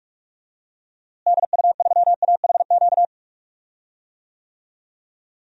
Morse code, heard as a keyed tone, about 46 wpm, 700 Hz. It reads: DU3AHQ
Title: DXCC-100 Call Signs - 6 Characters - Repeated 2x - 45wpm